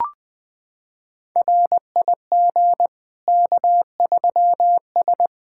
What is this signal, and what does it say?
Morse code, heard as a keyed tone, about 20 wpm, 700 Hz. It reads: RIG K3S
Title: K1USN SST QSO Practice 20wpm